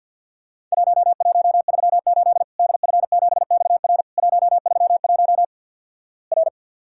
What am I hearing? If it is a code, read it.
Morse code, heard as a keyed tone, about 50 wpm, 700 Hz. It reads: 1138 DF7XR 131 R
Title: Worked All Europe DX Contest - QTC Exchange - 50wpm